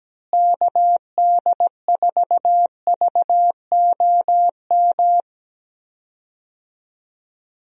Morse code, heard as a keyed tone, about 17 wpm, 700 Hz. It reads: KD4VOM
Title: USA Call Signs - 6 Characters - Repeated 3x - 17wpm